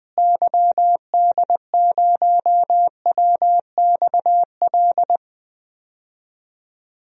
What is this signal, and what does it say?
Morse code, heard as a keyed tone, about 20 wpm, 700 Hz. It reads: YD0WXL
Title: DXCC-100 Call Signs - 6 Characters - Repeated 2x - 20wpm